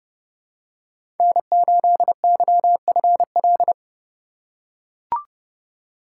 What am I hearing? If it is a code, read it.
Morse code, heard as a keyed tone, about 29 wpm, 700 Hz. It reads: N8YFL
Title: USA Call Signs - 5 Characters - 4x Character Spacing - 30wpm